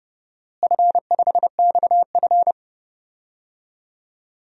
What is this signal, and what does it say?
Morse code, heard as a keyed tone, about 30 wpm, 700 Hz. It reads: F5XF
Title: DXCC-100 Call Signs - 4 Characters - Repeated 3x - 30wpm